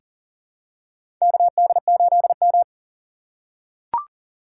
Morse code, heard as a keyed tone, about 40 wpm, 700 Hz. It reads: KD8M
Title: USA Call Signs - 4x Character Spacing - 40wpm